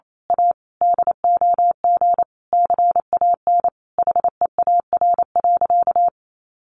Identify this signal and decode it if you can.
Morse code, heard as a keyed tone, about 28 wpm, 700 Hz. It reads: A DOG CAN HEAR.
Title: A Sound Of Thunder 28wpm